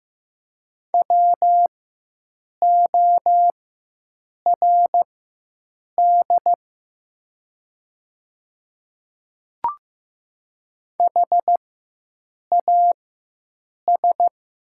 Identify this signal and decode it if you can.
Morse code, heard as a keyed tone, about 15 wpm, spaced as for 7 wpm, 700 Hz. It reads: WORD HAS
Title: 4x Character Spacing - Top 300 Words - 15wpm